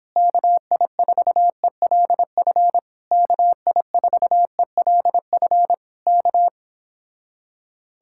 Morse code, heard as a keyed tone, about 26 wpm, 700 Hz. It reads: KI4ELF KI4ELF K